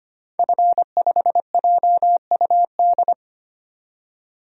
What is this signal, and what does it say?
Morse code, heard as a keyed tone, about 25 wpm, 700 Hz. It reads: F5JUD
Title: DXCC-100 Call Signs - 5 Characters - 25wpm